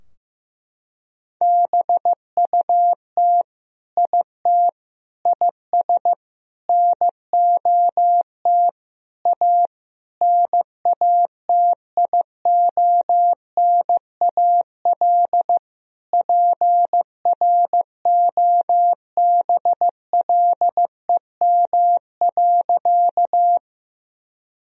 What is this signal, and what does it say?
Morse code, heard as a keyed tone, about 15 wpm, 700 Hz. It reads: BUT IT IS NOT A NATIONAL PROBLEM.
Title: Sentences from Top 1000 Words - Review Part 4 - 15wpm